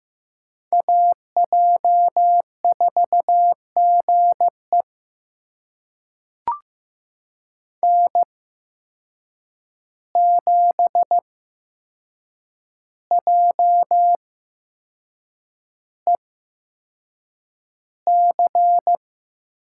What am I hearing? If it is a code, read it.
Morse code, heard as a keyed tone, about 15 wpm, spaced as for 6 wpm, 700 Hz. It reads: AJ4GE N7JEC
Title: USA Call Signs - 5 Characters - 8x Character Spacing - 15wpm